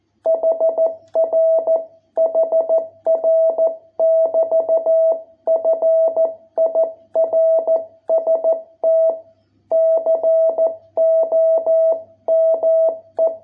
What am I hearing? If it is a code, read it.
Morse code, heard as a keyed tone, about 13 wpm, 660 Hz. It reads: HR HR <BT> FIRST COME